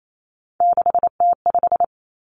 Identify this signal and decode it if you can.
Morse code, heard as a keyed tone, about 28 wpm, 700 Hz. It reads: 6T5
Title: Three Challenging Letter-Number Combinations 28wpm